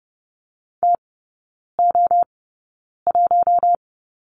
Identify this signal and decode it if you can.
Morse code, heard as a keyed tone, about 28 wpm, spaced as for 9 wpm, 690 Hz. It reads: TO1
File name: Lesson 16e - Letter c - Three Characters 30wpm